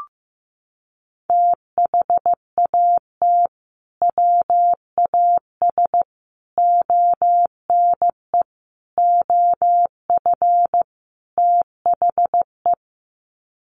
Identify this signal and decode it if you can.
Morse code, heard as a keyed tone, about 15 wpm, 700 Hz. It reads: THAT WAS ONE OF THE
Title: Sets of 5 Words 15wpm